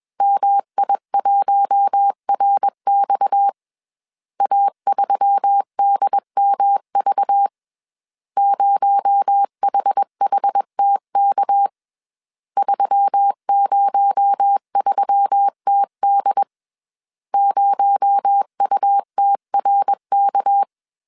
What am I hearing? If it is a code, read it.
Morse code, heard as a keyed tone, about 21 wpm, 800 Hz. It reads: MI1RX A3DMV 0HHTK 303TD 0UTRK
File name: CW / DET / 1 6 1500C S A 20wpm 2048